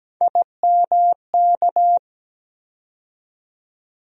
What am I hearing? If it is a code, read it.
Morse code, heard as a keyed tone, about 17 wpm, 700 Hz. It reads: IMK
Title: USA Call Sign Postfixes - 3 Letters 17wpm